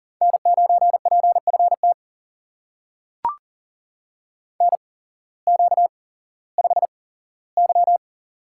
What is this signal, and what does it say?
Morse code, heard as a keyed tone, about 40 wpm, spaced as for 19 wpm, 700 Hz. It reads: N9PFT NQ5Y
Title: USA Call Signs - 8x Character Spacing - 40wpm